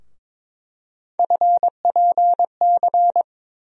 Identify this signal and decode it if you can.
Morse code, heard as a keyed tone, about 22 wpm, 700 Hz. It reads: FPC
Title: USA Call Sign Postfixes - 3 Letters 22wpm